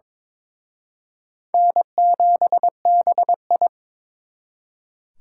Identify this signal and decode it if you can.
Morse code, heard as a keyed tone, about 26 wpm, 700 Hz. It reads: N7BI
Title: Speed Racing - USA 1x2 Call Signs 22wpm